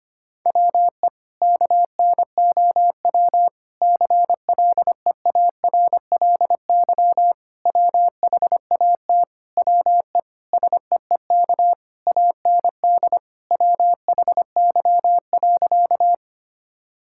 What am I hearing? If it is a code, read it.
Morse code, heard as a keyed tone, about 25 wpm, 700 Hz. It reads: WE KNOW CLEARLY WHAT WE SEEK AND WHY.
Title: Sentences from Top 1600 Words 25wpm